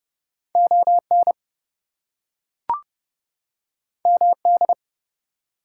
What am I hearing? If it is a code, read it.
Morse code, heard as a keyed tone, about 30 wpm, 700 Hz. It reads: ON MD
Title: US States and Canadian Provinces 30wpm